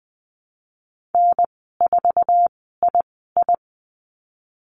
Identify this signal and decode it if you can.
Morse code, heard as a keyed tone, about 20 wpm, spaced as for 15 wpm, 700 Hz. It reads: N4II
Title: 048 Lesson 9g - Number 4 - Call Signs 20wpm